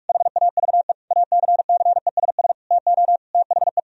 Morse code, heard as a keyed tone, about 41 wpm, 700 Hz. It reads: HAVE ACCESS TO THE
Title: Sentences from Top 1000 Words - Review Part 4 - 45wpm